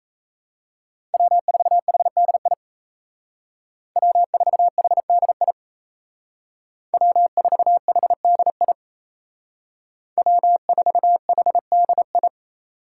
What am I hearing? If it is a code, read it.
Morse code, heard as a keyed tone, about 35 wpm, 700 Hz. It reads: W4HDI W4HDI W4HDI W4HDI